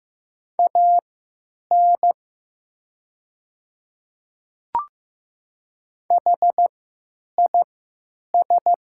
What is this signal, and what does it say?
Morse code, heard as a keyed tone, about 15 wpm, spaced as for 9 wpm, 700 Hz. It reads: AN HIS
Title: 3x Character Spacing - Top 200 Words - 15wpm